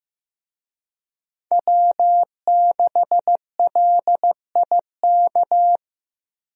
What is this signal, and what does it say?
Morse code, heard as a keyed tone, about 15 wpm, 700 Hz. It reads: W6LIK
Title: USA Call Signs - 5 Characters - 15wpm